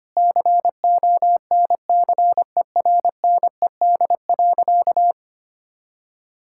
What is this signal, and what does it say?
Morse code, heard as a keyed tone, about 25 wpm, 700 Hz. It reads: CONCERNED.
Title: Sentences from Top 1000 Words - Review Part 6 - 25wpm